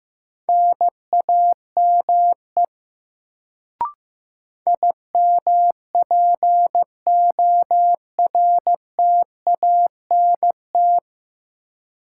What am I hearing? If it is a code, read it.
Morse code, heard as a keyed tone, about 15 wpm, 700 Hz. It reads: NAME IMPORTANT
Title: Sets of 3 Words - New Word Drill - 15wpm